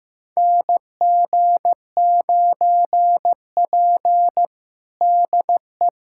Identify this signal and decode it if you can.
Morse code, heard as a keyed tone, about 15 wpm, 700 Hz. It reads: NG9P DE
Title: USA Callsign DE Callsign - 4 Characters - 15wpm